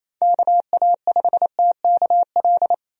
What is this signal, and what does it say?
Morse code, heard as a keyed tone, about 27 wpm, 700 Hz. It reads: KA5TKL
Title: USA Call Signs - 6 Characters - 28wpm